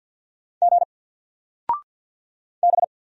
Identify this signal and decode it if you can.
Morse code, heard as a keyed tone, about 48 wpm, 700 Hz. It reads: G B
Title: Single Character (Letter:Number:Punctuation:Prosign) Alt 50wpm